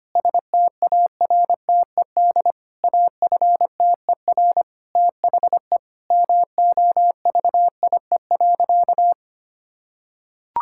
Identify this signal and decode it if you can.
Morse code, heard as a keyed tone, about 25 wpm, 700 Hz. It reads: STARTED AFTER THE MOVIE.
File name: practice / 25wpm / Sentences with 2-Word Phrases - 25wpm